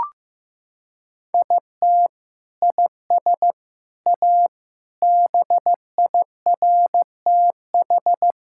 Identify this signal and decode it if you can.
Morse code, heard as a keyed tone, about 15 wpm, 700 Hz. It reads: IT IS A BIRTH
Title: Sentences from Top 1000 Words - Review Part 5 - 15wpm